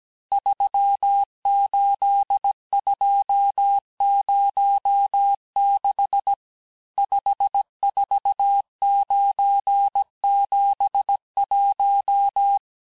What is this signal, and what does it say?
Morse code, heard as a keyed tone, about 17 wpm, 800 Hz. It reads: 38206 54971
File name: corrige-cours-1581-texte-chiffre-vitesse-16-mpm